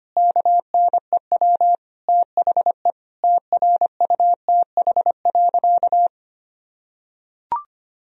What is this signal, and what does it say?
Morse code, heard as a keyed tone, about 25 wpm, 700 Hz. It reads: KNEW THE TRUTH.
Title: Sentences from Top 1000 Words - Review Part 1 - 25wpm